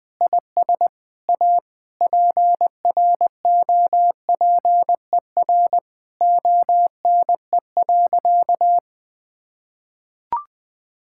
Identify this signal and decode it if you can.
Morse code, heard as a keyed tone, about 20 wpm, 700 Hz. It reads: IS A PROPER ONE.
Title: Sentences from Top 1000 Words - Review Part 5 - 20wpm